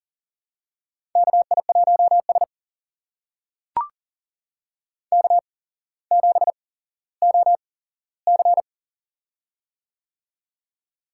Morse code, heard as a keyed tone, about 37 wpm, spaced as for 17 wpm, 700 Hz. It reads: KI1S K7OC
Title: USA Call Signs - 4 Characters - 8x Character Spacing - 40wpm